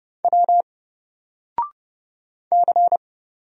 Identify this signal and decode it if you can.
Morse code, heard as a keyed tone, about 29 wpm, 700 Hz. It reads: W C
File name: Single-Letter-Number-30wpm